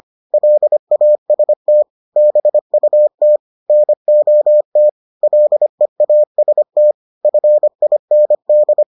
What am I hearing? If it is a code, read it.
Morse code, heard as a keyed tone, about 25 wpm, 600 Hz. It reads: LAST BUT NOT LEAST FIND